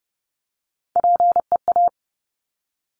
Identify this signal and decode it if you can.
Morse code, heard as a keyed tone, about 30 wpm, 700 Hz. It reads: PEA
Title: Top Three Letter Combinations in Words 30wpm